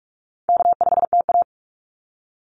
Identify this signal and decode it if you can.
Morse code, heard as a keyed tone, about 44 wpm, 700 Hz. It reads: K5TA
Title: Easy Call Signs - 45wpm